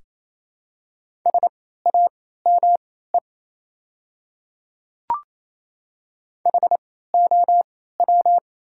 Morse code, heard as a keyed tone, about 28 wpm, spaced as for 16 wpm, 700 Hz. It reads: SAME HOW
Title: 3x Character Spacing - Top 500 Words - 28wpm